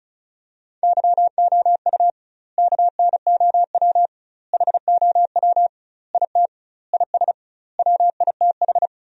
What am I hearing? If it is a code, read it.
Morse code, heard as a keyed tone, about 36 wpm, 700 Hz. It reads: YOU KNOW HOW IT IS WITH
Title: 2x Word-Spacing - Sentences from Top 100 Words - 35wpm